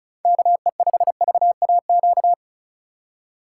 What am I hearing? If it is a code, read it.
Morse code, heard as a keyed tone, about 36 wpm, 700 Hz. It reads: KE5VAQ
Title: USA Call Signs - 6 Characters - 2x Character Spacing - 35wpm